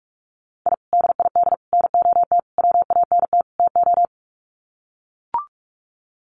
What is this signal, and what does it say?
Morse code, heard as a keyed tone, about 44 wpm, 700 Hz. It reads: I DID NOT WANT TO
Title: Sets of 5 Words Encore 45wpm